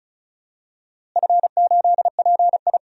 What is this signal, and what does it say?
Morse code, heard as a keyed tone, about 36 wpm, 700 Hz. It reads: F8PI
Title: DXCC-100 Call Signs - 4 Characters - Repeated 2x - 35wpm